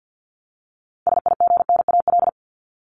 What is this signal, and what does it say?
Morse code, heard as a keyed tone, about 50 wpm, 700 Hz. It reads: SIGNAL
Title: Top 1251-1500 Words 50wpm